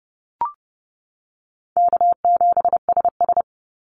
Morse code, heard as a keyed tone, about 31 wpm, 700 Hz. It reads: K7SS
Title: Call Signs - 30wpm